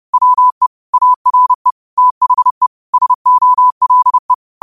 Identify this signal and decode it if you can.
Morse code, heard as a keyed tone, about 30 wpm, 1000 Hz. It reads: WE ARE THE SOLE
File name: qotdfiles / 30wpm